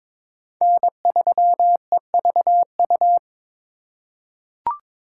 Morse code, heard as a keyed tone, about 22 wpm, 700 Hz. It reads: N3EVU
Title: USA Call Signs - 5 Characters - 2x Character Spacing - 22wpm